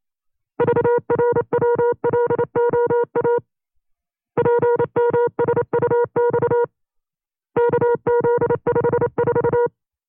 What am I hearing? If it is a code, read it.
Morse code, heard as a keyed tone, about 28 wpm, 480 Hz. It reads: VRWLOA PMSUX KZ54
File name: FreeFiles / CW-practice / 25WPM-side2